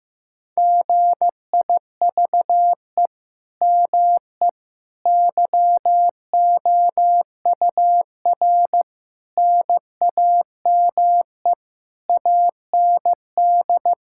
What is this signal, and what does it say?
Morse code, heard as a keyed tone, about 15 wpm, 700 Hz. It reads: GIVE ME YOUR NAME AND
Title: Sentences from Top 1000 Words - Review Part 6 - 15wpm